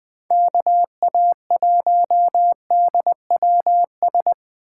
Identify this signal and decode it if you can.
Morse code, heard as a keyed tone, about 20 wpm, 700 Hz. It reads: KA1DWS
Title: Speed Racing - USA Call Signs 17wpm